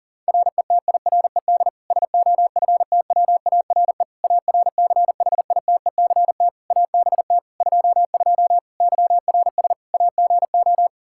Morse code, heard as a keyed tone, about 40 wpm, 700 Hz. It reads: RETIRED SOFTWARE ARCHITECT ABT 22 YRS AGO
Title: Practice QSO - Exchange 3 - More Info - 40wpm